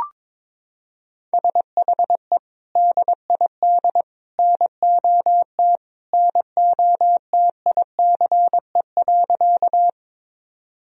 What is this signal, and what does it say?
Morse code, heard as a keyed tone, about 22 wpm, 700 Hz. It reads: SHE DID NOT NOTICE.
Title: Sentences from Top 1000 Words - Review Part 2 - 22wpm